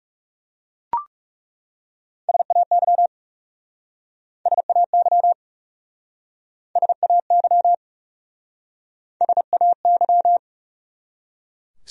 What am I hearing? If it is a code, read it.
Morse code, heard as a keyed tone, about 39 wpm, 700 Hz. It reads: SAY SAY SAY SAY